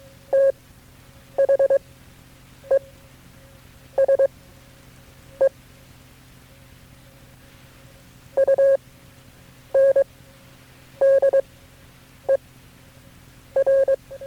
Morse code, heard as a keyed tone, about 22 wpm, spaced as for 7 wpm, 540 Hz. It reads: THESE UNDER
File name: KB6MT-tape3-side1